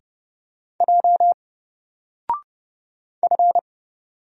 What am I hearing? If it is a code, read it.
Morse code, heard as a keyed tone, about 30 wpm, 700 Hz. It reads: J F
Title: Single Letter-Number Alt 30wpm